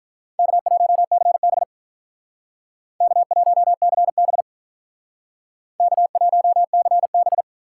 Morse code, heard as a keyed tone, about 45 wpm, 700 Hz. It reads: K1CB K1CB K1CB